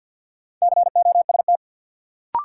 Morse code, heard as a keyed tone, about 49 wpm, 700 Hz. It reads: COST
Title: Sentences from Top 800 Words - New Word Drill - 50wpm